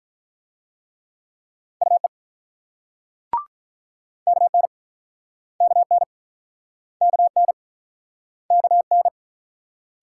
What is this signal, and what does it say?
Morse code, heard as a keyed tone, about 48 wpm, 700 Hz. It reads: UE KN KN KN KN